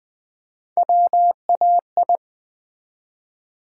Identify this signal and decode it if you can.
Morse code, heard as a keyed tone, about 20 wpm, 700 Hz. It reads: WAI
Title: Build A Word - 1 Letter At A Time - 4-Letter Words - 20wpm